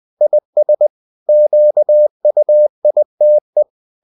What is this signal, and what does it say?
Morse code, heard as a keyed tone, about 20 wpm, 600 Hz. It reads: IS QUITE
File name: the old man and the sea / the old man and the sea part 1 20wpm